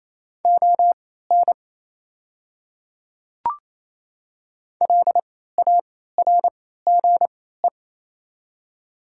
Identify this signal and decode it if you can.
Morse code, heard as a keyed tone, about 28 wpm, spaced as for 16 wpm, 690 Hz. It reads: ON LARGE
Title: 3x Character Spacing - Top 200 Words - 28wpm